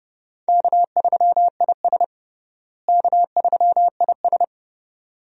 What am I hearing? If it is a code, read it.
Morse code, heard as a keyed tone, about 29 wpm, 700 Hz. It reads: K3IS K3IS
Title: USA Call Signs - 4 Characters - Repeated 2x - 30wpm